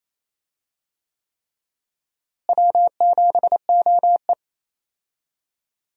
Morse code, heard as a keyed tone, about 28 wpm, 700 Hz. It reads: W7OE
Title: USA Call Signs - 4 Characters - Repeated 3x - 28wpm